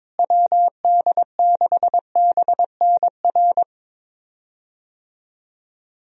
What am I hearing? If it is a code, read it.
Morse code, heard as a keyed tone, about 22 wpm, 700 Hz. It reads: WD6BNR
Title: USA Call Signs - 6 Characters - Repeated 2x - 22wpm